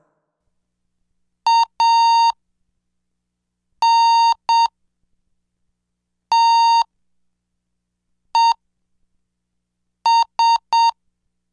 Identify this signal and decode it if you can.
Morse code, heard as a keyed tone, about 7 wpm, spaced as for 4 wpm, 920 Hz. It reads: ANTES